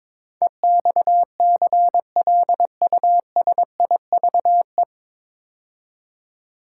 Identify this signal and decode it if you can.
Morse code, heard as a keyed tone, about 22 wpm, 700 Hz. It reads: EXCLUSIVE
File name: 9 Letter Words - First 3 Letters Given - 22wpm